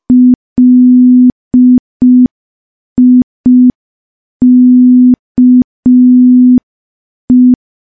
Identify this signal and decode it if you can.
Morse code, heard as a keyed tone, about 5 wpm, 250 Hz. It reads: LIKE
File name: morse / morse-training-05WPM